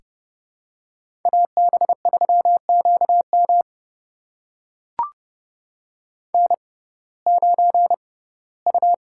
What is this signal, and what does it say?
Morse code, heard as a keyed tone, about 30 wpm, spaced as for 18 wpm, 700 Hz. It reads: AB3QM N9U
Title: USA Call Signs - 5 Characters - 6x Character Spacing - 30wpm